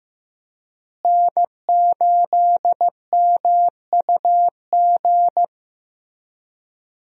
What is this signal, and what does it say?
Morse code, heard as a keyed tone, about 15 wpm, 700 Hz. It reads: N8MUG
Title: USA Call Signs - 5 Characters - Repeated 3x - 15wpm